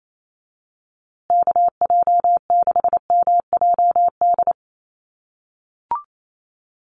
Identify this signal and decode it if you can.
Morse code, heard as a keyed tone, about 28 wpm, 690 Hz. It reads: KJ6MJD
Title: USA Call Signs Repeated 3x 28wpm